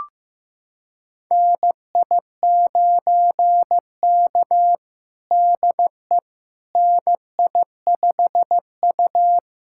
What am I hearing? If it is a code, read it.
Morse code, heard as a keyed tone, about 15 wpm, 700 Hz. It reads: NI9K DE NI5U
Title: USA Callsign DE Callsign - 4 Characters - 15wpm